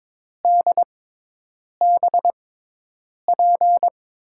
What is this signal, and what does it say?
Morse code, heard as a keyed tone, about 22 wpm, spaced as for 8 wpm, 700 Hz. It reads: DBP
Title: USA Call Signs - 6x Character Spacing - 22wpm